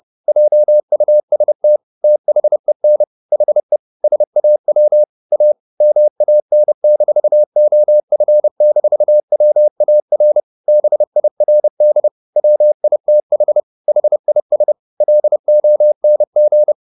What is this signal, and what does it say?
Morse code, heard as a keyed tone, about 30 wpm, 600 Hz. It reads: JUST THEN HE SAW A MAN-OF-WAR BIRD WITH HIS LONG
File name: the old man and the sea / the old man and the sea 1 30wpm